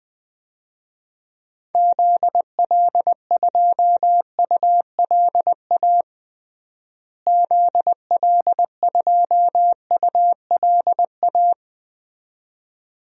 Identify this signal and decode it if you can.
Morse code, heard as a keyed tone, about 20 wpm, 700 Hz. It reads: ZL2ULA ZL2ULA
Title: DXCC-100 Call Signs - 6 Characters - Repeated 2x - 20wpm